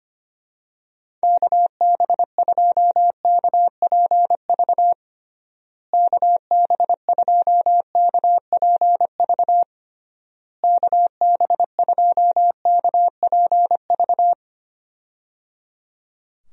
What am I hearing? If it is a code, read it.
Morse code, heard as a keyed tone, about 25 wpm, 700 Hz. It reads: KB2KPV KB2KPV KB2KPV
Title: USA Call Signs - 6 Characters - Repeated 3x - 25wpm